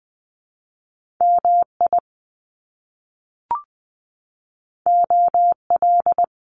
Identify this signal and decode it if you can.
Morse code, heard as a keyed tone, about 20 wpm, 700 Hz. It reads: MI OL
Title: Top Two Letter Combinations in Words 20wpm